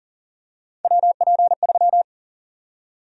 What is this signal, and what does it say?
Morse code, heard as a keyed tone, about 38 wpm, 700 Hz. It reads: WP3
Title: USA Call Sign Prefixes - Repeated 2x - 40wpm